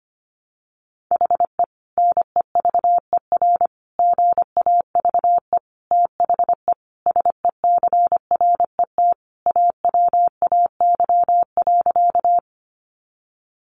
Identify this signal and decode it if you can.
Morse code, heard as a keyed tone, about 25 wpm, 700 Hz. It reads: HE NEVER GAVE THE SECRET AWAY.
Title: Sentences from Top 1200 Words 25wpm